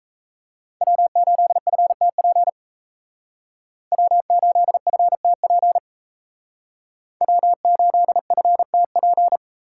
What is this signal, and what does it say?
Morse code, heard as a keyed tone, about 35 wpm, 700 Hz. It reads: W8FTP W8FTP W8FTP
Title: Speed Racing - USA 1x3 Call Signs 28wpm